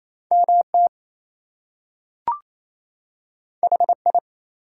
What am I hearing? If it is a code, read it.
Morse code, heard as a keyed tone, about 28 wpm, 700 Hz. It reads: MT HI
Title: US States and Canadian Provinces 28wpm